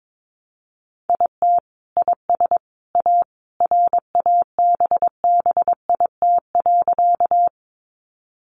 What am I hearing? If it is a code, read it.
Morse code, heard as a keyed tone, about 22 wpm, 700 Hz. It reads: IT IS A RABBIT.
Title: Easy Phrases 22wpm